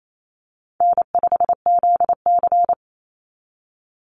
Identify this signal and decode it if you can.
Morse code, heard as a keyed tone, about 28 wpm, 700 Hz. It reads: N5ZC
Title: NA SOTA Chaser Callsigns 28wpm